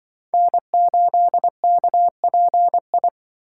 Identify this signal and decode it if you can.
Morse code, heard as a keyed tone, about 24 wpm, 700 Hz. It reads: N8KPI
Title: Speed Racing - USA 1x3 Call Signs 20wpm